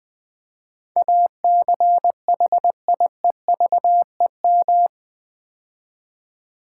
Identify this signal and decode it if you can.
Morse code, heard as a keyed tone, about 20 wpm, 700 Hz. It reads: ACHIEVEM
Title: Build A Word - 4 Letters At A Time - 11-Letter Words - 20wpm